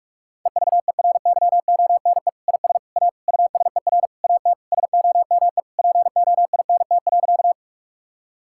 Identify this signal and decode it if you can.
Morse code, heard as a keyed tone, about 41 wpm, 700 Hz. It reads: EVERYONE IS A USER AT SOME POINT.
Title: Sentences from Top 1000 Words - Review Part 5 - 45wpm